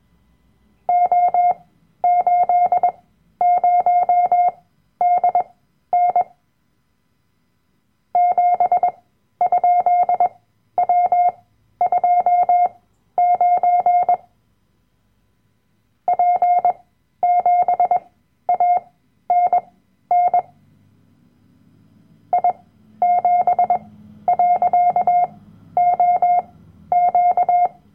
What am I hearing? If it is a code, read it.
Morse code, heard as a keyed tone, about 21 wpm, spaced as for 12 wpm, 710 Hz. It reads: O80DN 7?W29 P7ANN I7.OQ
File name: Workshop Dateien / CW-Online-Kurs / Lektion 010